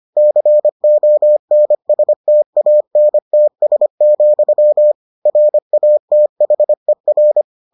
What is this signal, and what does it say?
Morse code, heard as a keyed tone, about 25 wpm, 600 Hz. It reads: CONSTANTS, RATHER